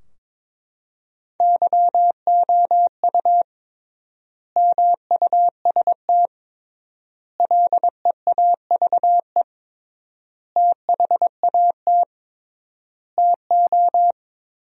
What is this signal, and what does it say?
Morse code, heard as a keyed tone, about 22 wpm, 700 Hz. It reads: YOU MUST LEAVE THAT TO
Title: 3x Word-Spacing - Sentences from Top 400 Words - 22wpm